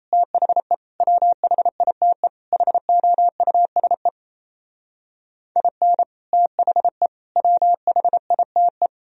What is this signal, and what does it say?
Morse code, heard as a keyed tone, about 33 wpm, 700 Hz. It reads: THE WHITE HOUSE IN THE WHITE
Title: Speed-Racing - Sets of 4 Words 28wpm